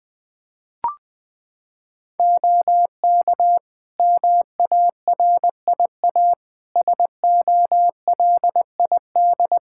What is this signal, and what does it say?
Morse code, heard as a keyed tone, about 21 wpm, 700 Hz. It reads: OK MARIA SOLID
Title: Practice QSO - Exchange 3 - More Info - 20wpm